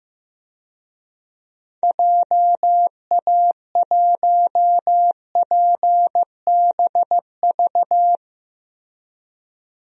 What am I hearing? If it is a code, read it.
Morse code, heard as a keyed tone, about 15 wpm, 700 Hz. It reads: JA1PBV
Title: DXCC-100 Call Signs - 6 Characters - Repeated 2x - 15wpm